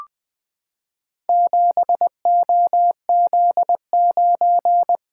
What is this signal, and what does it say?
Morse code, heard as a keyed tone, about 20 wpm, 700 Hz. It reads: 7OZ9
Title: Four Letter-Number Combinations - 20wpm